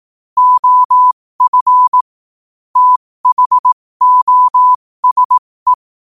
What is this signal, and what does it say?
Morse code, heard as a keyed tone, about 18 wpm, spaced as for 15 wpm, 1000 Hz. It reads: OF THOSE
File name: qotdfiles / 15wpm